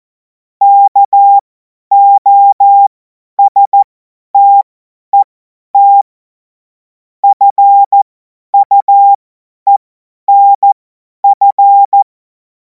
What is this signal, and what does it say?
Morse code, heard as a keyed tone, about 14 wpm, spaced as for 10 wpm, 800 Hz. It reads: KOSTET FUENF
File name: Morsekurs / Witze / Witz 19. Mai 2026 50 F70